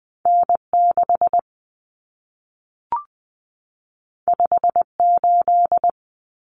Morse code, heard as a keyed tone, about 20 wpm, 700 Hz. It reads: N6 58
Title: Common Cut Numbers - 2-Digit 20wpm